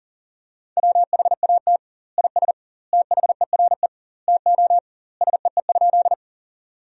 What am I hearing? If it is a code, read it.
Morse code, heard as a keyed tone, about 40 wpm, 700 Hz. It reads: WHAT IS THERE TO SEE?
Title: 2x Word-Spacing - Sentences from Top 100 Words - 40wpm